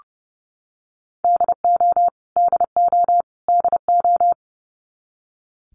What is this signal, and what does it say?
Morse code, heard as a keyed tone, about 30 wpm, 700 Hz. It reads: DO DO DO
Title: Top 200 Words Repeated 30wpm